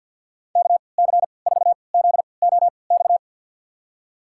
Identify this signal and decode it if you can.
Morse code, heard as a keyed tone, about 47 wpm, spaced as for 28 wpm, 700 Hz. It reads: KC4ZOX
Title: USA Call Signs - 6 Characters - 3x Character Spacing - 50wpm